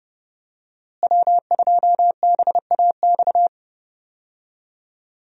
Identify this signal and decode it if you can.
Morse code, heard as a keyed tone, about 30 wpm, 700 Hz. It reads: W2BAX
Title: USA Call Signs - 5 Characters - Repeated 2x - 30wpm